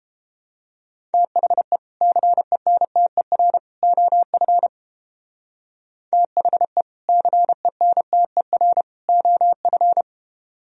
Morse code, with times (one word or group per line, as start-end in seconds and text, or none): 1.1-1.8 THE
2.0-3.6 CENTER
3.8-4.7 OF
6.1-6.8 THE
7.1-8.8 CENTER
9.1-10.0 OF